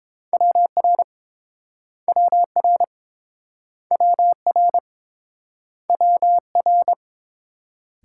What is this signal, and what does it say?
Morse code, heard as a keyed tone, about 31 wpm, 700 Hz. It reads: WR WR WR WR